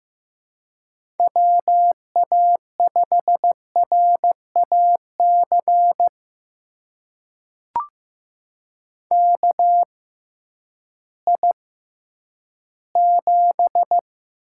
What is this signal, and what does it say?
Morse code, heard as a keyed tone, about 15 wpm, spaced as for 10 wpm, 700 Hz. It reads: WA5RAC KI7
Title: USA Call Signs - 6 Characters - 6x Character Spacing - 15wpm